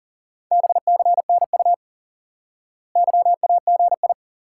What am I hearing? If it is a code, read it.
Morse code, heard as a keyed tone, about 37 wpm, 700 Hz. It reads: BCNU YAGI
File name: course / instant-qso-element-course-3 / IQE Course 3 - Lesson 89 - YAGI - Two Elements - 40wpm